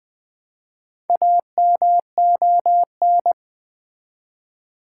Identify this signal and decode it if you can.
Morse code, heard as a keyed tone, about 20 wpm, 700 Hz. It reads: AMON
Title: Build A Word - 1 Letter At A Time - 3 to 5-Letter Words - 20wpm